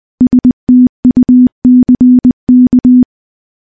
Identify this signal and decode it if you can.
Morse code, heard as a keyed tone, about 20 wpm, 250 Hz. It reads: STUCK
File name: morse / morse-training-20WPM